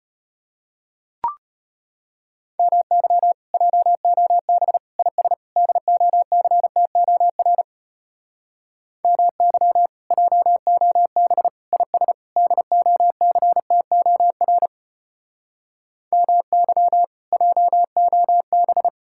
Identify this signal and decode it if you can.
Morse code, heard as a keyed tone, about 33 wpm, 700 Hz. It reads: MY JOB IS DOCTOR MY JOB IS DOCTOR MY JOB